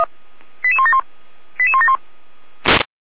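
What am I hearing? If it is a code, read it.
Five-tone selective call (ZVEI), heard as quick runs of five tones, tones of about 70 ms. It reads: 88171 88171
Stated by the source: Received by SDR, ZVEI five-tone call 88179.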